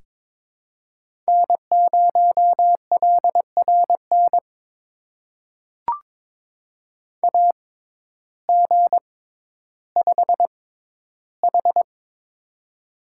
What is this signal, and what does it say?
Morse code, heard as a keyed tone, about 22 wpm, spaced as for 12 wpm, 700 Hz. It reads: N0LRN AG5H
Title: USA Call Signs - 6x Character Spacing - 22wpm